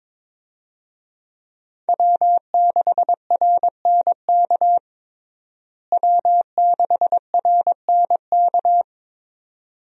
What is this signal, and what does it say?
Morse code, heard as a keyed tone, about 22 wpm, 700 Hz. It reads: W6RNK W6RNK
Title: USA Call Signs - 5 Characters - Repeated 2x - 22wpm